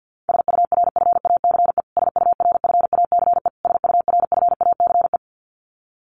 Morse code, heard as a keyed tone, about 49 wpm, 700 Hz. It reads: SURFACE SURFACE SURFACE
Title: Top 500 Words Repeated 50wpm